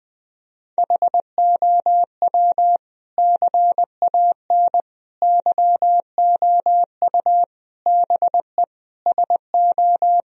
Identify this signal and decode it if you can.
Morse code, heard as a keyed tone, about 20 wpm, 700 Hz. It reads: HOW CAN YOU BE SO
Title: Sentences from Top 1000 Words - Review Part 1 - 20wpm